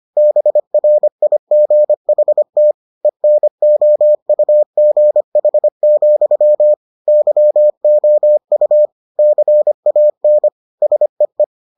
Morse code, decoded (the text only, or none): BRIGHT ENOUGH, YOU CAN SEE